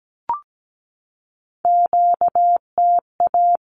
Morse code, heard as a keyed tone, about 17 wpm, 700 Hz. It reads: QTA
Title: Q-Codes 17wpm